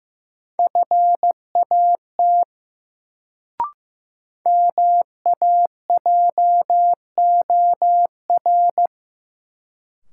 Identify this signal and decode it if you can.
Morse code, heard as a keyed tone, about 15 wpm, 700 Hz. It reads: FAT MAJOR
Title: Sentences from Top 600 Words - New Word Drill - 15wpm